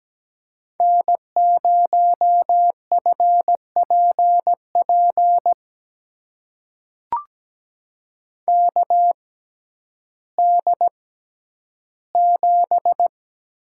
Morse code, heard as a keyed tone, about 17 wpm, spaced as for 10 wpm, 700 Hz. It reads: N0FPP KD7
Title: USA Call Signs - 5 Characters - 6x Character Spacing - 17wpm